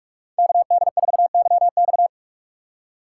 Morse code, heard as a keyed tone, about 44 wpm, 700 Hz. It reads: KD4YX